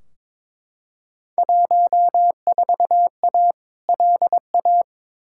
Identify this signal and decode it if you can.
Morse code, heard as a keyed tone, about 22 wpm, 700 Hz. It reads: 14A LA
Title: Field Day Exchanges Encore - 22wpm